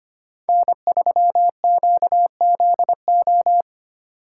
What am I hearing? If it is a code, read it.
Morse code, heard as a keyed tone, about 28 wpm, 700 Hz. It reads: N3QZO
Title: Speed Racing - USA 1x3 Call Signs 25wpm